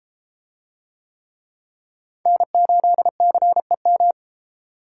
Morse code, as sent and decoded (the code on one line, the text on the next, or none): -. ---.. -.-. --
N8CEM